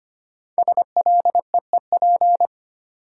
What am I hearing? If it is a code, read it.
Morse code, heard as a keyed tone, about 25 wpm, 700 Hz. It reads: SLEEP
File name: 5 Letter Words - First 3 Letters Given - 25wpm